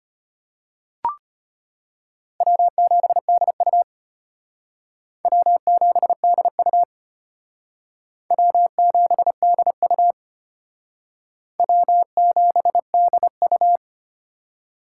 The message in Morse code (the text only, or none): W7DU W7DU W7DU W7DU